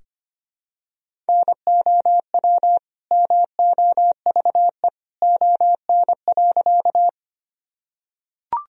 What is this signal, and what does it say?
Morse code, heard as a keyed tone, about 25 wpm, 700 Hz. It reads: NOW MOVE ON.
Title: Sentences from Top 1000 Words - Review Part 1 - 25wpm